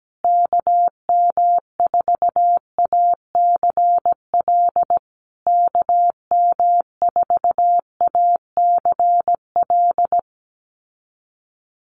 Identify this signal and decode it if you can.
Morse code, heard as a keyed tone, about 17 wpm, 700 Hz. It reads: KM4ACL KM4ACL
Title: USA Call Signs Repeated 3x 17wpm